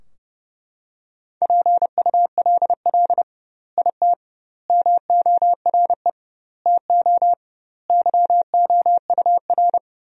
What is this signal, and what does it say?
Morse code, heard as a keyed tone, about 32 wpm, 700 Hz. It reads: PULL IT MORE TO YOUR
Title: 2x Word-Spacing - Sentences from Top 400 Words - 30wpm